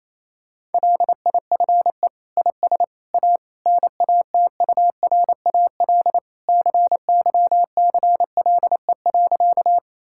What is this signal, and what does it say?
Morse code, heard as a keyed tone, about 27 wpm, 700 Hz. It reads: LIFE IS A NATURAL CYCLE.
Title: Sentences from Top 1000 Words - Review Part 6 - 28wpm